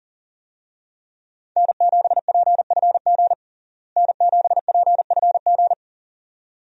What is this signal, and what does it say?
Morse code, heard as a keyed tone, about 40 wpm, 700 Hz. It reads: N7PFG N7PFG
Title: USA Call Signs - 5 Characters - Repeated 2x - 40wpm